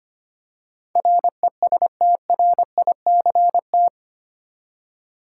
Morse code, heard as a keyed tone, about 25 wpm, 700 Hz. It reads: RESTRICT
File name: Eight Letter Words - 25wpm